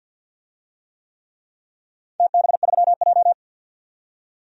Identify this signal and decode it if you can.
Morse code, heard as a keyed tone, about 53 wpm, 700 Hz. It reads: TB3J
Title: DXCC-100 Call Signs - 4 Characters - 50wpm